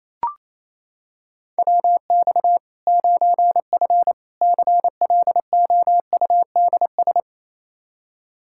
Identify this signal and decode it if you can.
Morse code, heard as a keyed tone, about 28 wpm, 700 Hz. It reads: WX 9F CLOUDS
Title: Practice QSO - Exchange 2 - Info - 28wpm